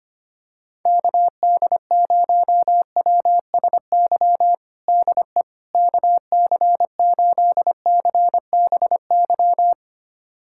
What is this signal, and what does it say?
Morse code, heard as a keyed tone, about 25 wpm, 700 Hz. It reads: KD0WSY DE KC8CBY
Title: USA Callsign DE Callsign 25wpm